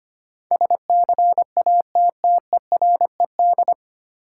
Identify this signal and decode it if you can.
Morse code, heard as a keyed tone, about 25 wpm, 700 Hz. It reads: SCATTERED
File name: Nine Letter Words - 25wpm